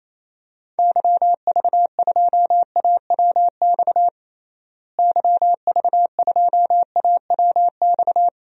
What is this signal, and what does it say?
Morse code, heard as a keyed tone, about 28 wpm, 700 Hz. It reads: YV2AWX YV2AWX
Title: DXCC-100 Call Signs - 6 Characters - Repeated 2x - 28wpm